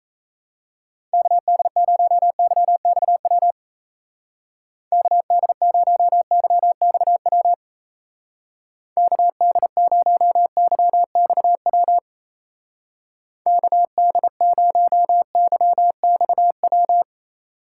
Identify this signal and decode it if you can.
Morse code, heard as a keyed tone, about 34 wpm, 700 Hz. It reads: KD0YXW KD0YXW KD0YXW KD0YXW
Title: Speed Racing - USA 2x3 Call Signs 28wpm